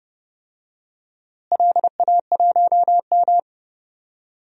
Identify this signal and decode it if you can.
Morse code, heard as a keyed tone, about 30 wpm, 700 Hz. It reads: LA1M
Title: DXCC-100 Call Signs - 4 Characters - Repeated 3x - 30wpm